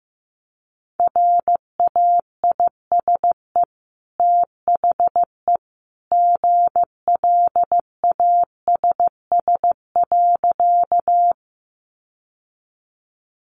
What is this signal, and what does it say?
Morse code, heard as a keyed tone, about 15 wpm, 700 Hz. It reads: RAISE THE GLASS.
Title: Sentences from Top 1400 Words 15wpm